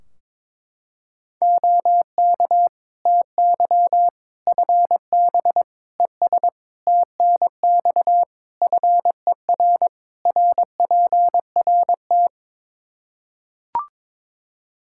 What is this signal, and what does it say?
Morse code, heard as a keyed tone, about 22 wpm, 700 Hz. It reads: OK TY FB ES TNX FER RPRT
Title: Practice QSO - Exchange 2 - Info - 22wpm